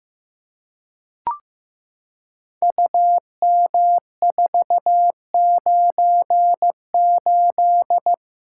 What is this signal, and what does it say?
Morse code, heard as a keyed tone, about 15 wpm, 700 Hz. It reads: UM498
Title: Code Groups - 5 Letter:Numbers - 15wpm